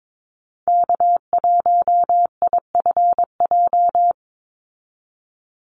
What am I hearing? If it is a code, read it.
Morse code, heard as a keyed tone, about 22 wpm, 700 Hz. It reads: K1IFJ
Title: Call Signs Repeated 3x - 22wpm